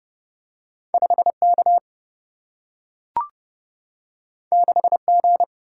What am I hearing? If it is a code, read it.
Morse code, heard as a keyed tone, about 30 wpm, 700 Hz. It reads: 5K 6G
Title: Single-Letter Single-Number 30wpm